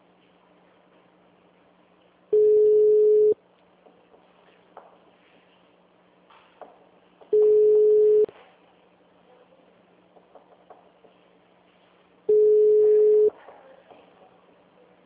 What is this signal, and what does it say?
Ringback tone (Europe, 425 Hz).